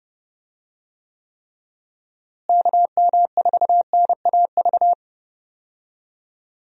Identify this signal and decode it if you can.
Morse code, heard as a keyed tone, about 30 wpm, 700 Hz. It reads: KM4NAV
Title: USA Call Signs - 6 Characters - Repeated 3x - 30wpm